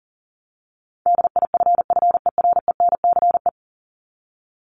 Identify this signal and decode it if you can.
Morse code, heard as a keyed tone, about 40 wpm, 700 Hz. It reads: DIFFERENCE
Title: Top 751-1000 Words 40wpm